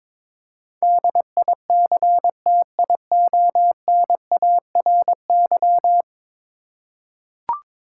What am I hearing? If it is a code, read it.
Morse code, heard as a keyed tone, about 21 wpm, 700 Hz. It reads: DICTIONARY
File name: practice / 22wpm / 10 Letter Words - First 3 Letters Given - 22wpm